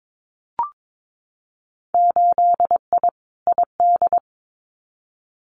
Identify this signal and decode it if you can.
Morse code, heard as a keyed tone, about 22 wpm, 700 Hz. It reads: 8I ID
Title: Winter Field Day Exchanges Encore 22wpm